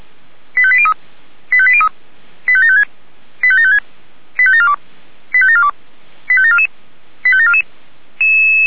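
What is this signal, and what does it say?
Five-tone selective call (ZVEI), heard as quick runs of five tones, tones of about 70 ms. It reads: 85803 85803 85756 85756 85742 85742 85744 85744
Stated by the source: ZVEI five-tone call 85729 received by SDR.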